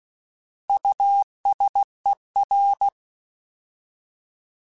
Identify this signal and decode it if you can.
Morse code, heard as a keyed tone, about 16 wpm, 800 Hz. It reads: USER
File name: Alphabetical / Alphabetical mp3s / Us @15